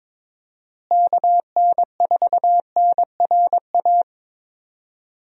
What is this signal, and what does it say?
Morse code, heard as a keyed tone, about 22 wpm, 700 Hz. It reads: KN4NRA
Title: USA Call Signs - 6 Characters - Repeated 3x - 22wpm